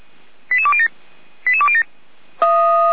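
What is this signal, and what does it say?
Five-tone selective call (ZVEI), heard as quick runs of five tones, tones of about 80 ms. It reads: 88207 88207
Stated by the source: ZVEI five-tone selective call 88205 received by SDR, Feuerwehr Orlamünde fire brigade.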